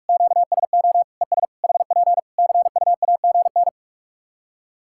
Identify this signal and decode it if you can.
Morse code, heard as a keyed tone, about 46 wpm, 700 Hz. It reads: QSO ES HP CUAGN
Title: Practice QSO - Exchange 4 - Ending - 45wpm